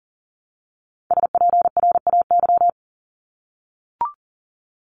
Morse code, heard as a keyed tone, about 38 wpm, 700 Hz. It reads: SPRAY
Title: Top 1751-2000 Words 40wpm